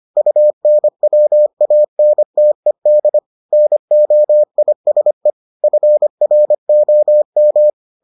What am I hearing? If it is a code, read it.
Morse code, heard as a keyed tone, about 25 wpm, 600 Hz. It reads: UNWANTED NOISE FROM